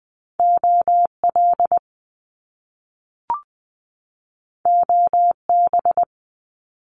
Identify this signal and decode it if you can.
Morse code, heard as a keyed tone, about 20 wpm, 700 Hz. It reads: OL OB
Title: Top Two Letter Combinations in Words 20wpm